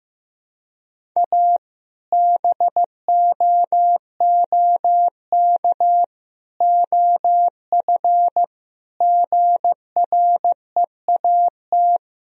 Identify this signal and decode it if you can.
Morse code, heard as a keyed tone, about 15 wpm, 700 Hz. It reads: A BOOK OF GREAT
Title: Sentences from Top 1000 Words - Review Part 2 - 15wpm